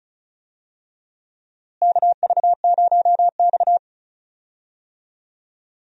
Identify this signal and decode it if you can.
Morse code, heard as a keyed tone, about 34 wpm, 700 Hz. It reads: KV0X
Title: USA Call Signs - 4 Characters - Repeated 2x - 35wpm